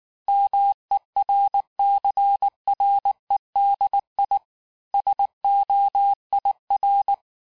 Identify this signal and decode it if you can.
Morse code, heard as a keyed tone, about 19 wpm, 800 Hz. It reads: MERCREDI SOIR
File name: cours-du-mercredi / audios / cours-869-texte-CHIFFRE-vitesse-18-MPM